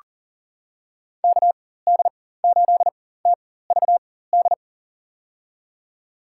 Morse code, heard as a keyed tone, about 40 wpm, spaced as for 19 wpm, 690 Hz. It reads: KD8TVD
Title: USA Call Signs - 4x Character Spacing - 40wpm